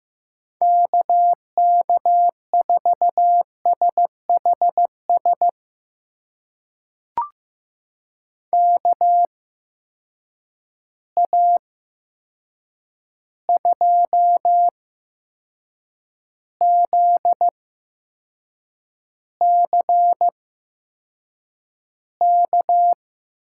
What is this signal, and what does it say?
Morse code, heard as a keyed tone, about 16 wpm, spaced as for 6 wpm, 700 Hz. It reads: KK4SHS KA2ZCK
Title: USA Call Signs - 6 Characters - 8x Character Spacing - 15wpm